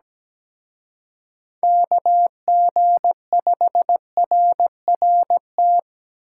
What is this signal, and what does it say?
Morse code, heard as a keyed tone, about 20 wpm, 700 Hz. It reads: KG5RRT
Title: Speed Racing - USA Call Signs 17wpm